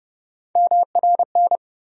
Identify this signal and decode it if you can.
Morse code, heard as a keyed tone, about 30 wpm, 700 Hz. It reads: MRN
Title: USA Call Sign Postfixes - 3 Letters 30wpm